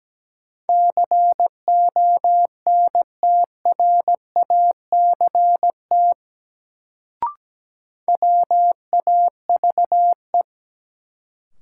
Morse code, heard as a keyed tone, about 17 wpm, 700 Hz. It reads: CONTRACT WAVE
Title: Sentences from Top 1400 Words - New Word Drill Alt - 17wpm